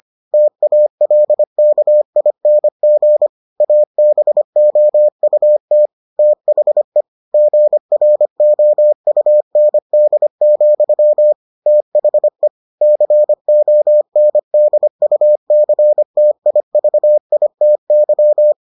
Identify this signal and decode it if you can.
Morse code, heard as a keyed tone, about 25 wpm, 600 Hz. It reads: TALKING ABOUT THE GROUND, THE CONDUCTIVITY